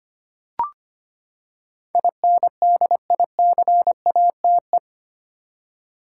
Morse code, heard as a keyed tone, about 25 wpm, 700 Hz. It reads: INDICATE
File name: Eight Letter Words - 25wpm